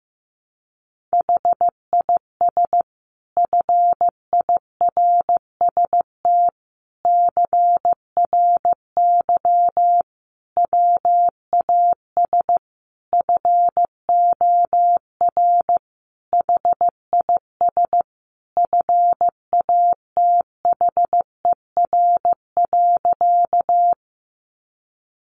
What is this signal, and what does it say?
Morse code, heard as a keyed tone, about 15 wpm, 700 Hz. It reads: HIS FIRST CRY WAS FOR HIS FATHER.
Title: Sentences from Top 500 Words 15wpm